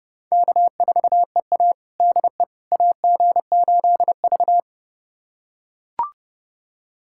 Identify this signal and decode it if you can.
Morse code, heard as a keyed tone, about 30 wpm, 700 Hz. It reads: K4EA DE AG8V
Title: USA Callsign DE Callsign - 4 Characters - 30wpm